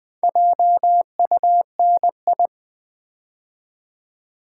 Morse code, heard as a keyed tone, about 20 wpm, 700 Hz. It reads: JUNI
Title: Build A Word - 2 Letters At A Time - 6-Letter Words - 20wpm